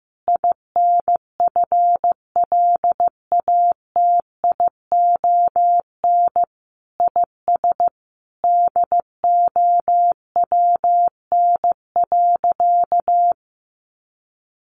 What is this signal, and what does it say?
Morse code, heard as a keyed tone, about 15 wpm, 700 Hz. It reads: INFLATION IS DOWN.
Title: Sentences from Top 1100 Words 15wpm